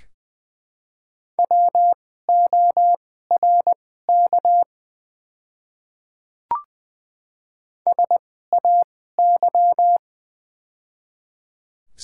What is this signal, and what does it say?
Morse code, heard as a keyed tone, about 20 wpm, spaced as for 15 wpm, 700 Hz. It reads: WORK SAY
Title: 2x Character Spacing - Top 300 Words - 20wpm